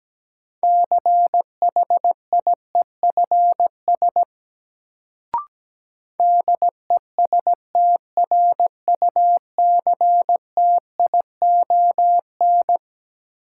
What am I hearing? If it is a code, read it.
Morse code, heard as a keyed tone, about 17 wpm, 700 Hz. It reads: CHIEFS DESTRUCTION
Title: Sets of 5 Words Encore - New Word Drill - 17wpm